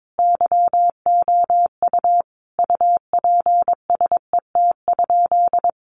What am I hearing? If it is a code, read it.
Morse code, heard as a keyed tone, about 22 wpm, 700 Hz. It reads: YOU UPSET?
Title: Sentences from Top 800 Words 22wpm